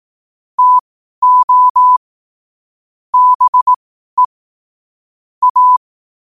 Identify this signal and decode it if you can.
Morse code, heard as a keyed tone, about 18 wpm, spaced as for 12 wpm, 1000 Hz. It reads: TO BE A
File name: qotdfiles / 12wpm